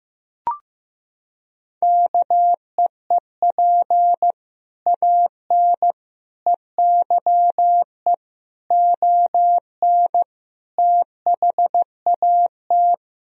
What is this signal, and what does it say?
Morse code, heard as a keyed tone, about 16 wpm, 700 Hz. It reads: KEEP AN EYE ON THAT
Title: Sentences from Top 1000 Words - Review Part 2 - 15wpm